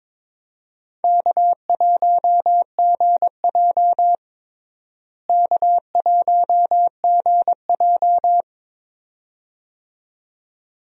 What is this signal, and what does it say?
Morse code, heard as a keyed tone, about 22 wpm, 700 Hz. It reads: K1GJ K1GJ
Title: USA Call Signs - 4 Characters - Repeated 2x - 22wpm